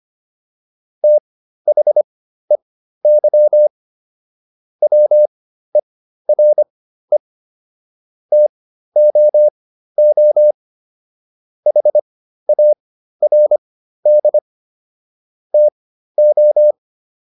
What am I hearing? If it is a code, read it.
Morse code, heard as a keyed tone, about 25 wpm, spaced as for 13 wpm, 600 Hz. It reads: THEY WERE TOO HARD TO